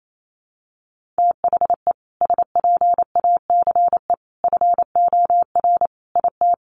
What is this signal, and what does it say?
Morse code, heard as a keyed tone, about 28 wpm, 700 Hz. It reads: THE SPACE FOR IT
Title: Sentences from Top 1500 Words 28wpm